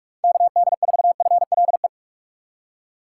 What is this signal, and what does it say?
Morse code, heard as a keyed tone, about 41 wpm, 700 Hz. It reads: KD4FLE
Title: USA Call Signs - 6 Characters - 6x Character Spacing - 45wpm